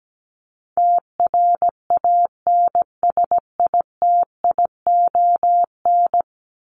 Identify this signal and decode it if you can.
Morse code, heard as a keyed tone, about 17 wpm, 700 Hz. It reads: TRANSITION
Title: Top 1751-2000 Words 17wpm